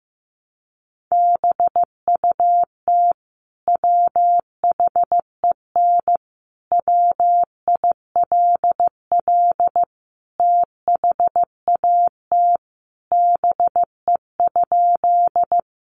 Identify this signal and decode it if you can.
Morse code, heard as a keyed tone, about 15 wpm, 700 Hz. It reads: BUT WHEN WILL THAT BE?
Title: Sentences from Top 100 Words 15wpm